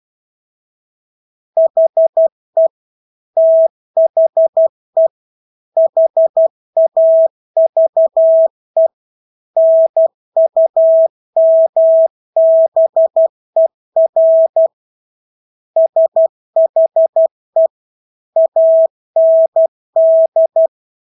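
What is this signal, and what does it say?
Morse code, heard as a keyed tone, about 12 wpm, 650 Hz. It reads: HE THE HAVE NUMBER SHE AND
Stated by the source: Morse practice of English words at 12 wpm.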